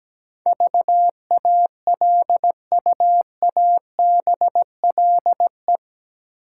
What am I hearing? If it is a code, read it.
Morse code, heard as a keyed tone, about 17 wpm, 700 Hz. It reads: VALUABLE
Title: Sentences from Top 1100 Words - New Word Drill - 17wpm